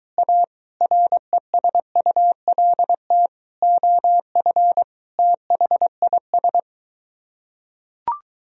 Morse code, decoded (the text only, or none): A RESULT OF THIS